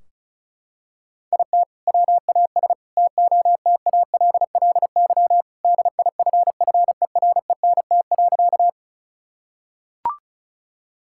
Morse code, heard as a keyed tone, about 34 wpm, 700 Hz. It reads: IT WAS TOTALLY DIFFERENT.
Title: Sentences from Top 1600 Words 35wpm